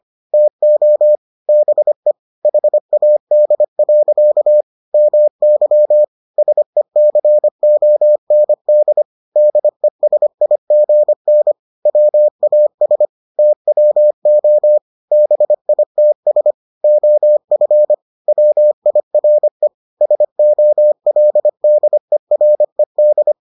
Morse code, heard as a keyed tone, about 25 wpm, 600 Hz. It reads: TO BE HAD. MY SECOND DESIGN WAS TWO BITS OF WIRE SOLDERED